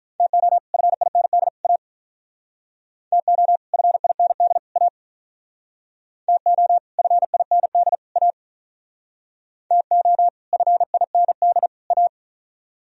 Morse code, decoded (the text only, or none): TO FIND A TO FIND A TO FIND A TO FIND A